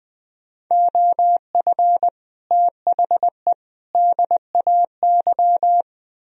Morse code, heard as a keyed tone, about 20 wpm, 700 Hz. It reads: OF THE DAY
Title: Speed-Racing - Sets of 3 Words 17wpm